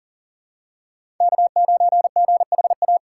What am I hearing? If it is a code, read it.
Morse code, heard as a keyed tone, about 40 wpm, 700 Hz. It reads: K9GHA
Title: USA Call Signs - 5 Characters - 4x Character Spacing - 40wpm